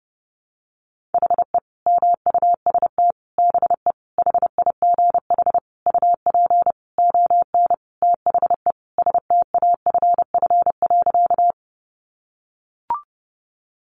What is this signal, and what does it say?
Morse code, heard as a keyed tone, about 30 wpm, 700 Hz. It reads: HE MUST BE HIGH UP ON THE STAFF.
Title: Sentences from Top 1200 Words 30wpm